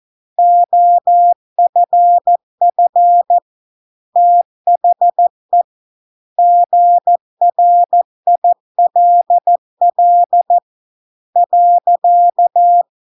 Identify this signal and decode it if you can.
Morse code, heard as a keyed tone, about 14 wpm, 700 Hz. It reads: OFF THE GRILL .